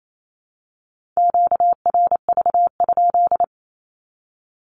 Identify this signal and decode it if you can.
Morse code, heard as a keyed tone, about 28 wpm, 700 Hz. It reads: QRV?
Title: Q-Codes 28wpm